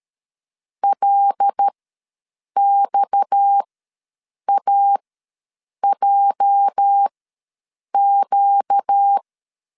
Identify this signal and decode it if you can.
Morse code, heard as a keyed tone, about 13 wpm, spaced as for 7 wpm, 800 Hz. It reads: LXAJQ
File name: LEZIONI / 9 28 ws 12wpm 3354